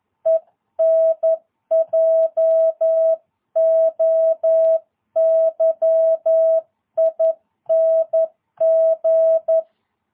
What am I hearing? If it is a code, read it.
Morse code, heard as a keyed tone, about 11 wpm, 660 Hz. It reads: ENJOYING
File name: jh8cbh / 2022cw